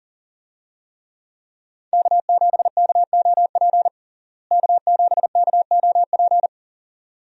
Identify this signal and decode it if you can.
Morse code, heard as a keyed tone, about 40 wpm, 700 Hz. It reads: K7KOP K7KOP
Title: USA Call Signs - 5 Characters - Repeated 2x - 40wpm